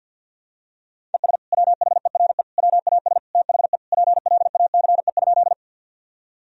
Morse code, heard as a keyed tone, about 52 wpm, 700 Hz. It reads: ES WHERE WAS THE PLACE?
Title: Sentences from Top 200 Words and Abbreviations 50wpm